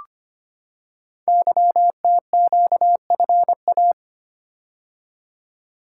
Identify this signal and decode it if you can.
Morse code, heard as a keyed tone, about 25 wpm, 700 Hz. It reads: YTQFA
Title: Code Groups - 5 Letters - 25wpm